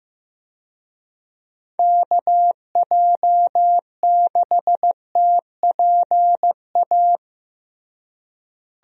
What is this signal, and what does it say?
Morse code, heard as a keyed tone, about 15 wpm, 700 Hz. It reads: KJ6TPA
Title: USA Call Signs - 6 Characters - Repeated 2x - 15wpm